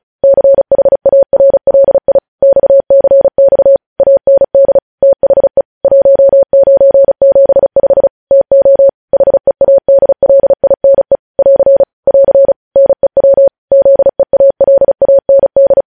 Morse code, heard as a keyed tone, about 36 wpm, 550 Hz. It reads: CHARLI XCX AND THE 1975 TO HEADLINE <AR> <AR> NEW ZEALAND
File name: podcast / News headlines-1x-35x35